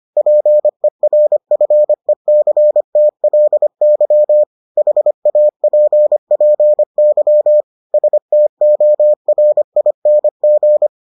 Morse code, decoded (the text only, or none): PERFECTLY HAPPY STORING